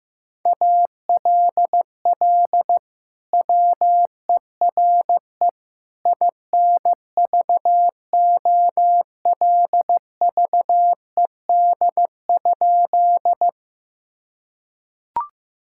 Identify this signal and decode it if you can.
Morse code, heard as a keyed tone, about 16 wpm, 700 Hz. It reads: ALL WERE INVOLVED?
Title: Sentences from Top 1000 Words - Review Part 5 - 15wpm